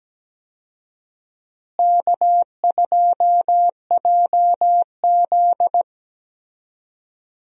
Morse code, heard as a keyed tone, about 17 wpm, 700 Hz. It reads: K2JZ
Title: USA Call Signs - 4 Characters - Repeated 3x - 17wpm